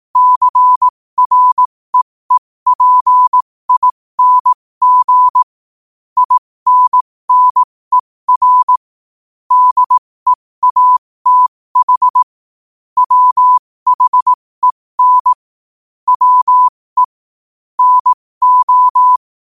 Morse code, heard as a keyed tone, about 18 wpm, spaced as for 15 wpm, 1000 Hz. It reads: CREEPING INNER DEATH WHEN WE NO